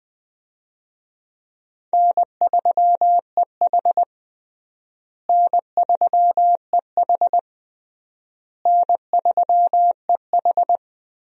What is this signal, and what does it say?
Morse code, heard as a keyed tone, about 20 wpm, 700 Hz. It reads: N3EH N3EH N3EH
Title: USA Call Signs - 4 Characters - Repeated 3x - 20wpm